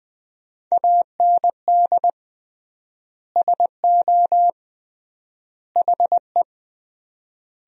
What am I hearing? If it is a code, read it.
Morse code, heard as a keyed tone, about 20 wpm, 700 Hz. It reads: AND SO HE
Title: 3x Word-Spacing - Sentences from Top 500 Words - 20wpm